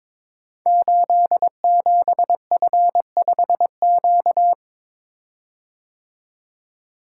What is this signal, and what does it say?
Morse code, heard as a keyed tone, about 22 wpm, 700 Hz. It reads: 87F5Q
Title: Code Groups - 5 Letter:Numbers - 22wpm